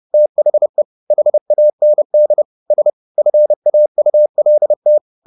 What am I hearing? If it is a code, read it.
Morse code, heard as a keyed tone, about 30 wpm, 600 Hz. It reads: THE HAND S FAULT
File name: the old man and the sea / the old man and the sea 1 30wpm